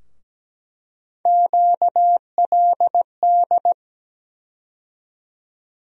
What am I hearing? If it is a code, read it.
Morse code, heard as a keyed tone, about 17 wpm, 700 Hz. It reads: QLD
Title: USA Call Sign Postfixes - 3 Letters 17wpm